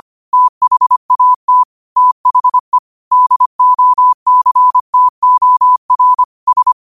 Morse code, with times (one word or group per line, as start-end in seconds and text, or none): 0.3-1.6 THAT
2.0-2.8 THE
3.1-6.2 DOCTOR
6.5-6.7 S